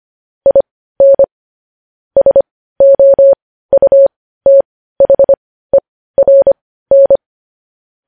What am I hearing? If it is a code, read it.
Morse code, heard as a keyed tone, about 25 wpm, spaced as for 15 wpm, 550 Hz. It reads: IN SOUTHERN
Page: news-headlines-in-morse-code-at-15-wpm